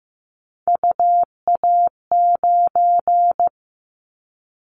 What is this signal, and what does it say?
Morse code, heard as a keyed tone, about 14 wpm, 700 Hz. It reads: UA9
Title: Callsign Prefixes for Top 25 DXCC - Repeated 3x - 15wpm